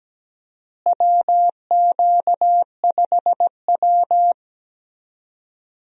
Morse code, heard as a keyed tone, about 17 wpm, 700 Hz. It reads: WQ5W
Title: USA Call Signs - 4 Characters - Repeated 2x - 17wpm